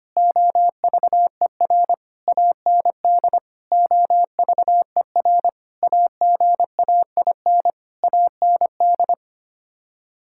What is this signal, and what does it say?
Morse code, heard as a keyed tone, about 25 wpm, 700 Hz. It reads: OVER AND OVER AGAIN AND
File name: Sets of 9 Words - 25wpm